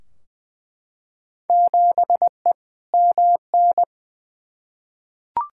Morse code, heard as a keyed tone, about 20 wpm, 700 Hz. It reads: 7E MN
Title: Field Day Exchanges Encore - 20wpm